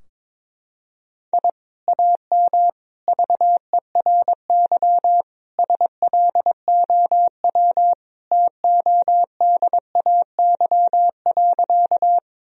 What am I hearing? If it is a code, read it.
Morse code, heard as a keyed tone, about 22 wpm, 700 Hz. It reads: I AM VERY SLOW TODAY.
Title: Sentences from Top 1000 Words - Review Part 2 - 22wpm